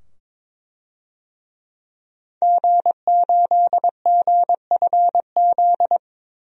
Morse code, heard as a keyed tone, about 21 wpm, 700 Hz. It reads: G8GFZ
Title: DXCC-100 Call Signs - 5 Characters - 22wpm